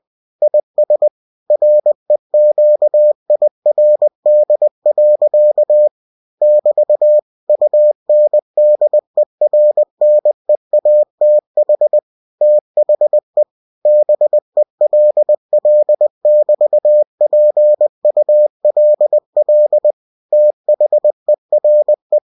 Morse code, heard as a keyed tone, about 20 wpm, 600 Hz. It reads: IS REQIRD. <BT> UNDERNEATH THE BELL<BT>PULL THERE
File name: cw-books / winnie-the-pooh / winnie-20wpm-page-0029